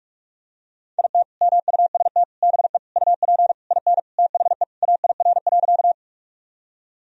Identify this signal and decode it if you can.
Morse code, heard as a keyed tone, about 48 wpm, 700 Hz. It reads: IT MUST BE UP IN THE AIR.
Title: Sentences from Top 1000 Words - Review Part 1 - 45wpm